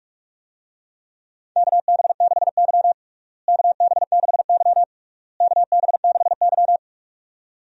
Morse code, heard as a keyed tone, about 41 wpm, 700 Hz. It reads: KB6Y KB6Y KB6Y
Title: USA Call Signs - 4 Characters - Repeated 3x - 45wpm